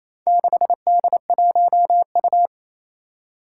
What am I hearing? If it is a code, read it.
Morse code, heard as a keyed tone, about 27 wpm, 700 Hz. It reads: 6D1U
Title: Four Challenging Letter-Number Combinations 28wpm